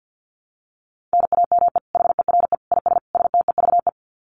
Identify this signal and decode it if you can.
Morse code, heard as a keyed tone, about 50 wpm, 700 Hz. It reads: NAME HERE IS STEVE
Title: Common QSO Phrases 50wpm